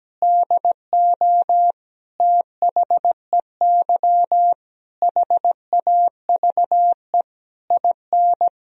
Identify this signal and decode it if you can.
Morse code, heard as a keyed tone, about 17 wpm, 700 Hz. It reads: DO THEY HAVE IN
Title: Sentences from Top 1000 Words - Review Part 2 - 17wpm